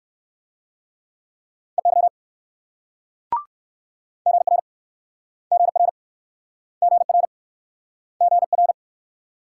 Morse code, heard as a keyed tone, about 56 wpm, 690 Hz. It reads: EY GR GR GR GR
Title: Speed-Racing Multi-letter Phonemes - 45wpm